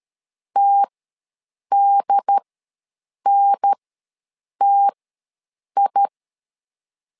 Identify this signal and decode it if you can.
Morse code, heard as a keyed tone, about 13 wpm, spaced as for 7 wpm, 800 Hz. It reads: TDNTI
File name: LEZIONI / 2 UHDN / 2 UHDN ws 12wpm 1708